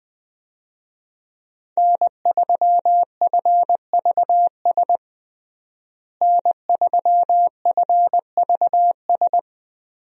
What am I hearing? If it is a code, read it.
Morse code, heard as a keyed tone, about 20 wpm, 700 Hz. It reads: N3FVS N3FVS
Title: USA Call Signs - 5 Characters - Repeated 2x - 20wpm